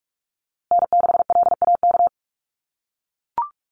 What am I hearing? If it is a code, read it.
Morse code, heard as a keyed tone, about 44 wpm, 700 Hz. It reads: N6LAK
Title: USA Call Signs Repeated 2x 45wpm